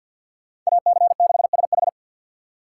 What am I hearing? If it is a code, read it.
Morse code, heard as a keyed tone, about 61 wpm, 700 Hz. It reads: AC6SH
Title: Speed Racing - USA 2x2 Call Signs 50wpm